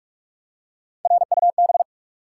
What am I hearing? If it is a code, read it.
Morse code, heard as a keyed tone, about 44 wpm, 700 Hz. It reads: RUB
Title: Three Letter Words - 45wpm